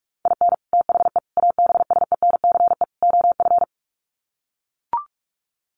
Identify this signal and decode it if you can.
Morse code, heard as a keyed tone, about 44 wpm, 700 Hz. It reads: IN THE ABSENCE OF
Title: Sets of 4 Words 45wpm